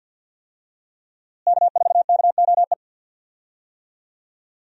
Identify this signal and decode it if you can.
Morse code, heard as a keyed tone, about 47 wpm, 700 Hz. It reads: K4KOE
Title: USA Call Signs - 5 Characters - Repeated 3x - 50wpm